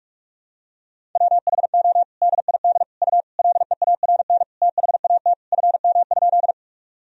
Morse code, heard as a keyed tone, about 46 wpm, 700 Hz. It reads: WHO DID U LEARN THAT FM?
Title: Sentences from Top 300 Words and Abbreviations 45wpm